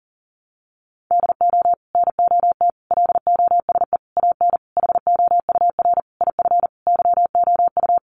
Morse code, heard as a keyed tone, about 40 wpm, 700 Hz. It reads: DO NOT LOSE AN HOUR IF YOU
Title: Sentences from Top 1100 Words 40wpm